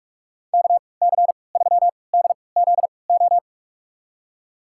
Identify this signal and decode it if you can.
Morse code, heard as a keyed tone, about 46 wpm, spaced as for 26 wpm, 700 Hz. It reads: KC3DZO
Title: USA Call Signs - 3x Character Spacing - 45wpm